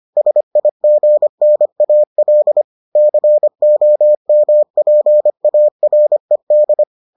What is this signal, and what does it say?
Morse code, heard as a keyed tone, about 25 wpm, 600 Hz. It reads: SIGNAL COMPARED